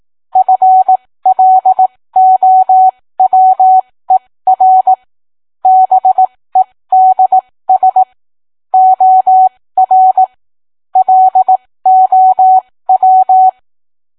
Morse code, heard as a keyed tone, about 18 wpm, spaced as for 15 wpm, 750 Hz. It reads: FLOWER BEDS OR LOW